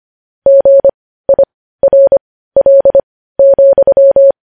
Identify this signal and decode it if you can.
Morse code, heard as a keyed tone, about 25 wpm, spaced as for 15 wpm, 550 Hz. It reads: GIRL,
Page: news-headlines-in-morse-code-at-15-wpm